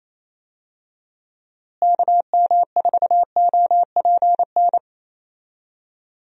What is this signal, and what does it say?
Morse code, heard as a keyed tone, about 28 wpm, 700 Hz. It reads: KM4OPN
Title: USA Call Signs - 6 Characters - Repeated 2x - 28wpm